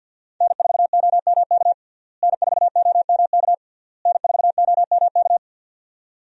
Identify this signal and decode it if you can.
Morse code, heard as a keyed tone, about 50 wpm, 700 Hz. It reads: N4OMK N4OMK N4OMK
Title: USA Call Signs - 5 Characters - Repeated 3x - 50wpm